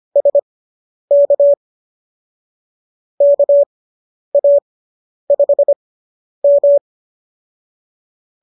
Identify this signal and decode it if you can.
Morse code, heard as a keyed tone, about 25 wpm, spaced as for 10 wpm, 570 Hz. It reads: SK KA5M